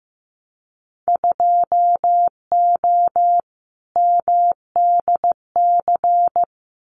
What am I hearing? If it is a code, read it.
Morse code, heard as a keyed tone, about 15 wpm, 700 Hz. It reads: 2O MDC
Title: Winter Field Day Exchanges Encore 15wpm